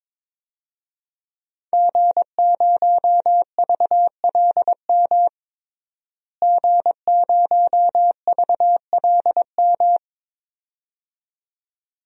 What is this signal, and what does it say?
Morse code, heard as a keyed tone, about 22 wpm, 700 Hz. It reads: G0VLM G0VLM
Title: DXCC-100 Call Signs - 5 Characters - Repeated 2x - 22wpm